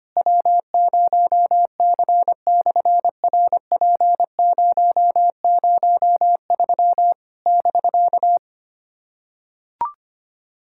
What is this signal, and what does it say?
Morse code, heard as a keyed tone, about 25 wpm, 700 Hz. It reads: W0C/RP003 <BK>
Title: SOTA QSO Practice 25wpm